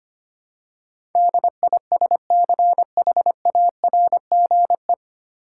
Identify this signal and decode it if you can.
Morse code, heard as a keyed tone, about 25 wpm, 700 Hz. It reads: DISCHARGE
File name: Nine Letter Words - 25wpm